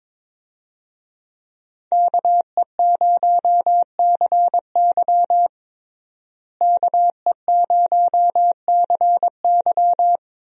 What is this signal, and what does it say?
Morse code, heard as a keyed tone, about 22 wpm, 700 Hz. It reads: KE0CY KE0CY
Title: USA Call Signs - 5 Characters - Repeated 2x - 22wpm